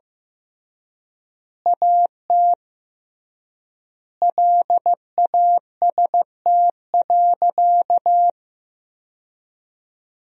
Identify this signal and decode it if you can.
Morse code, heard as a keyed tone, about 15 wpm, 700 Hz. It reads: AT LAST.
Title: 3x Word-Spacing - Sentences from Top 300 Words - 15wpm